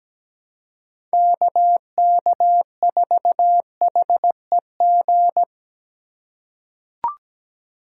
Morse code, heard as a keyed tone, about 17 wpm, 700 Hz. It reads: KK4HEG
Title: K1USN SST QSO Practice - 17wpm